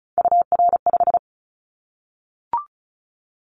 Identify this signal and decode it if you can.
Morse code, heard as a keyed tone, about 36 wpm, 700 Hz. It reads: UR5
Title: Callsign Prefixes for Top 25 DXCC - 35wpm